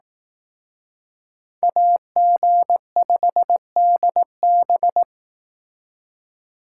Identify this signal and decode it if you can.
Morse code, heard as a keyed tone, about 19 wpm, 700 Hz. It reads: AG5DB
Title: Speed Racing - USA 2x2 Call Signs 15wpm